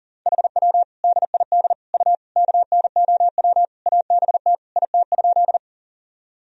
Morse code, heard as a keyed tone, about 40 wpm, 700 Hz. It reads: HW DID U KNOW ABT IT?
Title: Sentences from Top 100 Words and Abbreviations 40wpm